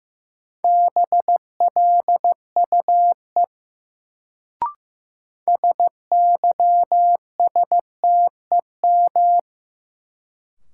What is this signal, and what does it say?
Morse code, heard as a keyed tone, about 16 wpm, 700 Hz. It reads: BLUE SYSTEM
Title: Sentences from Top 1000 Words - Review Part 3 - New Word Drill - 15wpm